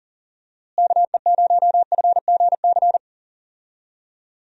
Morse code, heard as a keyed tone, about 40 wpm, 700 Hz. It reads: KE0FGC
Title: USA Call Signs - 6 Characters - Repeated 3x - 40wpm